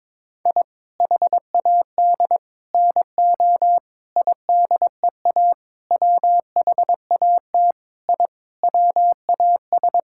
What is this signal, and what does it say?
Morse code, heard as a keyed tone, about 21 wpm, 700 Hz. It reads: I HAD NO IDEA WHAT I WAS
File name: practice / 22wpm / Sets of 8 Words - 22wpm